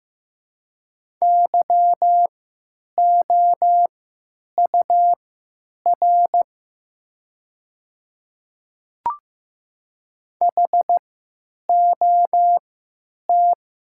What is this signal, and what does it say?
Morse code, heard as a keyed tone, about 15 wpm, spaced as for 9 wpm, 700 Hz. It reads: YOUR HOT
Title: 3x Character Spacing - Top 100 Words - 15wpm